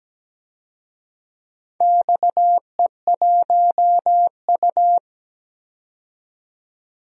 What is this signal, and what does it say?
Morse code, heard as a keyed tone, about 17 wpm, 700 Hz. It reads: XE1U
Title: DXCC-100 Call Signs - 4 Characters - 17wpm